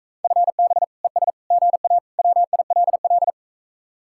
Morse code, heard as a keyed tone, about 43 wpm, 700 Hz. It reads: FB ES GA WILL